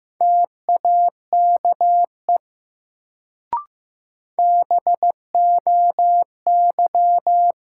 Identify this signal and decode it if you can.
Morse code, heard as a keyed tone, about 15 wpm, 700 Hz. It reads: TAKE BOY
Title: Sentences from Top 200 Words - New Word Drill - 15wpm